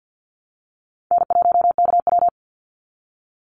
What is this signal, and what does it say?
Morse code, heard as a keyed tone, about 50 wpm, 700 Hz. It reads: N1KW
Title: Easy Call Signs Repeated 3x - 50wpm